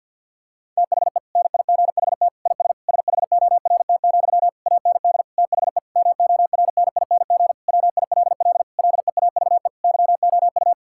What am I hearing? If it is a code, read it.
Morse code, heard as a keyed tone, about 50 wpm, 700 Hz. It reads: THE NIGHT IS SHORT, AND THE MORNING WILL LEAVE YOU
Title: Sentences from Top 1600 Words 50wpm